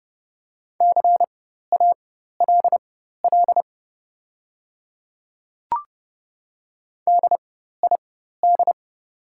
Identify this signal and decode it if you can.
Morse code, heard as a keyed tone, about 30 wpm, spaced as for 14 wpm, 700 Hz. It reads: CALL DID
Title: 4x Character Spacing - Top 200 Words - 30wpm